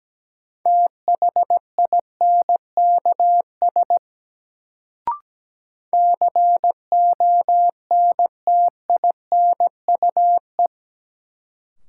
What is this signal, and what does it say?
Morse code, heard as a keyed tone, about 17 wpm, 700 Hz. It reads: THINKS CONTINUE
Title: Sentences from Top 1000 Words - Review Part 6 - New Word Drill - 17wpm